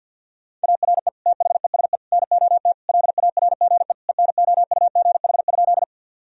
Morse code, heard as a keyed tone, about 52 wpm, 700 Hz. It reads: ARE THESE NOT LARGE ENOUGH?
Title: Sentences from Top 1000 Words - Review Part 1 - 50wpm